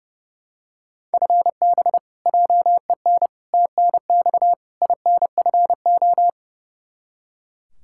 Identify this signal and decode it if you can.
Morse code, heard as a keyed tone, about 30 wpm, 700 Hz. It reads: FB JEN TNX INFO
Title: Practice QSO - Exchange 3 - More Info - 30wpm